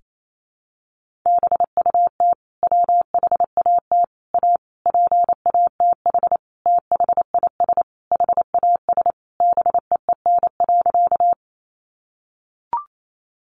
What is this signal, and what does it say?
Morse code, heard as a keyed tone, about 28 wpm, 700 Hz. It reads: BUT WHAT A PATH THIS HAS BEEN.
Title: Sentences from Top 1400 Words 28wpm